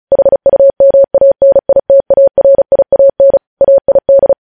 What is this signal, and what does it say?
Morse code, heard as a keyed tone, about 36 wpm, 550 Hz. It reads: HUMANITARIAN AID
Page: News Headlines in Morse Code at 35 WPM